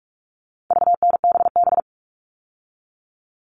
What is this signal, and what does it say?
Morse code, heard as a keyed tone, about 44 wpm, 700 Hz. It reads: VNBB
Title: All Cut Numbers - 4-Digit 45wpm